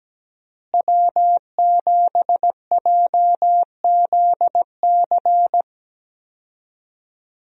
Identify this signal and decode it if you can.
Morse code, heard as a keyed tone, about 17 wpm, 700 Hz. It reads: W7JZC
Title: USA Call Signs - 5 Characters - Repeated 2x - 17wpm